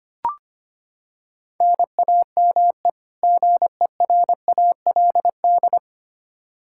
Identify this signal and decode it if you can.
Morse code, heard as a keyed tone, about 25 wpm, 700 Hz. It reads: NAME GERALD
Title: Practice QSO - Exchange 1 - Beginning - 25wpm